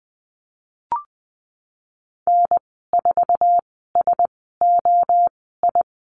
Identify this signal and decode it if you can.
Morse code, heard as a keyed tone, about 20 wpm, spaced as for 15 wpm, 700 Hz. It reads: N4SOI
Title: 048 Lesson 9g - Number 4 - Call Signs 20wpm